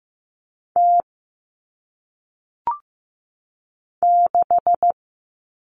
Morse code, heard as a keyed tone, about 15 wpm, 700 Hz. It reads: T 6
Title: All Cut Numbers - 1-Digit 15wpm